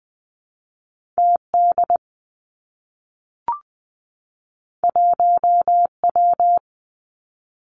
Morse code, heard as a keyed tone, about 20 wpm, 700 Hz. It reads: T D 1 W
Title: Two Challenging Letter-Number Combinations 20wpm